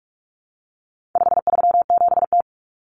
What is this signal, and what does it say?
Morse code, heard as a keyed tone, about 44 wpm, 700 Hz. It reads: 537T
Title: Common Cut Numbers - 4-Digit 45wpm